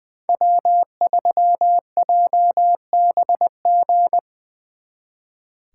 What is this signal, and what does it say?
Morse code, heard as a keyed tone, about 23 wpm, 700 Hz. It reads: W3JBG
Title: Speed Racing - USA 1x3 Call Signs 20wpm